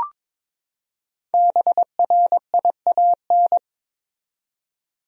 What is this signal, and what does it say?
Morse code, heard as a keyed tone, about 22 wpm, 700 Hz. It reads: BRIAN
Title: USA Common Names - 22wpm